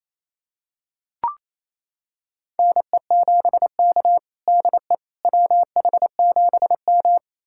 Morse code, heard as a keyed tone, about 28 wpm, 700 Hz. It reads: NE7K DE WH7M
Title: USA Callsign DE Callsign - 4 Characters - 28wpm